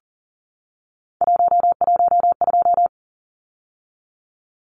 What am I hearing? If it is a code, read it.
Morse code, heard as a keyed tone, about 40 wpm, 700 Hz. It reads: 112
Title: Numbers - 3-Digit 40wpm